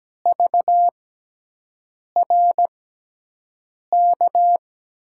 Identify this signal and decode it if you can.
Morse code, heard as a keyed tone, about 17 wpm, spaced as for 6 wpm, 700 Hz. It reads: VRK
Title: USA Call Signs - 5 Characters - 6x Character Spacing - 17wpm